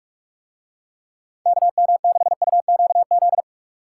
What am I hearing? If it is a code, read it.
Morse code, heard as a keyed tone, about 44 wpm, 700 Hz. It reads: KM6UQZ